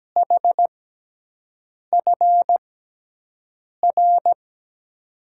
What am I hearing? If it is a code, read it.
Morse code, heard as a keyed tone, about 17 wpm, spaced as for 6 wpm, 700 Hz. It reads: HFR
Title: USA Call Signs - 6 Characters - 6x Character Spacing - 17wpm